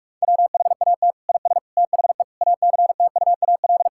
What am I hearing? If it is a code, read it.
Morse code, heard as a keyed tone, about 46 wpm, 700 Hz. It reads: WHAT IS THE ACTUAL
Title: Sentences from Top 1000 Words - Review Part 5 - 45wpm